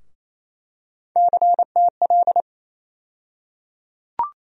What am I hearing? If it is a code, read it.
Morse code, heard as a keyed tone, about 28 wpm, 700 Hz. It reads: CTL
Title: USA Call Sign Postfixes - 3 Letters 28wpm